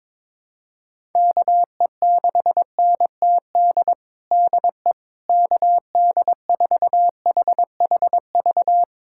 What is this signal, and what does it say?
Morse code, heard as a keyed tone, about 22 wpm, 700 Hz. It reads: KE6NTD DE KD4HHV
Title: USA Callsign DE Callsign 22wpm